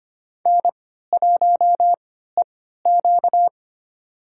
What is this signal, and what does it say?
Morse code, heard as a keyed tone, about 25 wpm, spaced as for 14 wpm, 700 Hz. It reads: N1EQ
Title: USA Call Signs - 3x Character Spacing - 25wpm